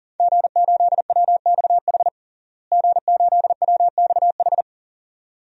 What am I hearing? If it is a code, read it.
Morse code, heard as a keyed tone, about 40 wpm, 700 Hz. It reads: G8WXH G8WXH
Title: DXCC-100 Call Signs - 5 Characters - Repeated 2x - 40wpm